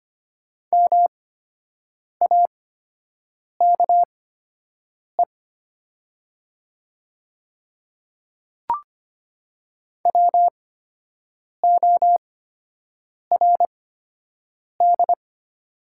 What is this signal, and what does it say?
Morse code, heard as a keyed tone, about 25 wpm, spaced as for 7 wpm, 700 Hz. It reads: MAKE WORD
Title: 8x Character Spacing - Top 100 Words - 25wpm